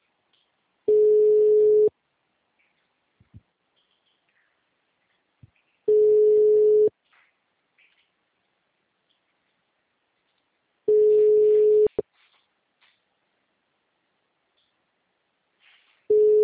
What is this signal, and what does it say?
Ringback tone (Europe, 425 Hz).
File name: out- -2008- - - .30757